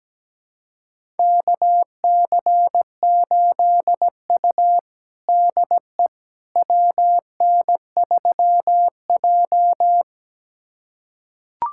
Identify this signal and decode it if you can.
Morse code, heard as a keyed tone, about 17 wpm, 700 Hz. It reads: KC8U DE WN3J
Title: USA Callsign DE Callsign - 4 Characters - 17wpm